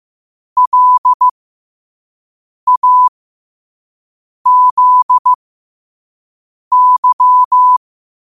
Morse code, heard as a keyed tone, about 15 wpm, spaced as for 5 wpm, 1000 Hz. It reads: LAZY
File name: qotdfiles / 5wpm